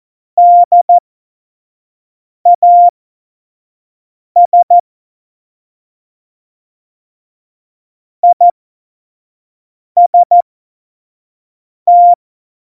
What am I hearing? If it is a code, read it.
Morse code, heard as a keyed tone, about 14 wpm, spaced as for 5 wpm, 700 Hz. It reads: DAS IST
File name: Morsekurs / Witze / Witz 03.Dezember 2019, 25 F70